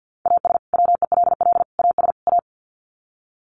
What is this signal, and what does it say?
Morse code, heard as a keyed tone, about 50 wpm, 700 Hz. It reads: AS WELL AS A
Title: Sets of 4 Words 50wpm